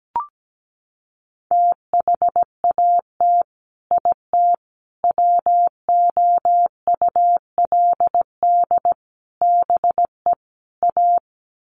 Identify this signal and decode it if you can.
Morse code, heard as a keyed tone, about 17 wpm, 700 Hz. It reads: THAT IT WOULD BE A
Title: Sets of 5 Words Encore2 17wpm